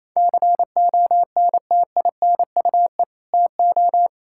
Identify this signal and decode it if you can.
Morse code, heard as a keyed tone, about 28 wpm, 700 Hz. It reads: CONTINUE TO
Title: Sentences from Top 1000 Words - Review Part 6 - 28wpm